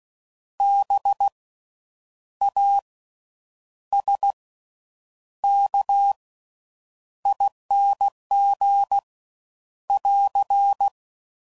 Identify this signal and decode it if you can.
Morse code, heard as a keyed tone, about 16 wpm, spaced as for 8 wpm, 800 Hz. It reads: BASKING <AR>
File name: Slow With Faster Ings